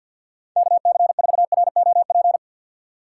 Morse code, heard as a keyed tone, about 52 wpm, 700 Hz. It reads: KC4ROP
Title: USA Call Signs - 4x Character Spacing - 50wpm